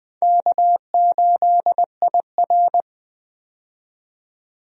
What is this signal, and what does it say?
Morse code, heard as a keyed tone, about 20 wpm, 700 Hz. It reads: K8IR
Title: USA Call Signs - 4 Characters - 20wpm